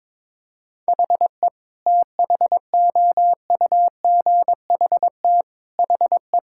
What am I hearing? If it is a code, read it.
Morse code, heard as a keyed tone, about 22 wpm, 700 Hz. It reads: HE THOUGHT HE
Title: Sentences from Top 1000 Words - Review Part 6 - 22wpm